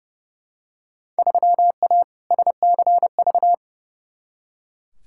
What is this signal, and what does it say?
Morse code, heard as a keyed tone, about 30 wpm, 700 Hz. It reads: 3A SCV
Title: Field Day Exchanges - 30wpm